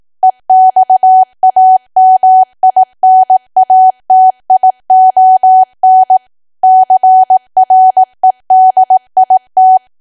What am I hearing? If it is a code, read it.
Morse code, heard as a keyed tone, about 18 wpm, 750 Hz. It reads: EXAMINATION CREDIT
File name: cwa / W1AW-Level2 / 18WPM